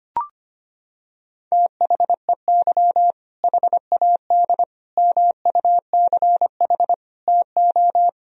Morse code, heard as a keyed tone, about 25 wpm, 700 Hz. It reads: THEY HAD MUCH TO
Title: Sentences from Top 400 Words and Abbreviations 25wpm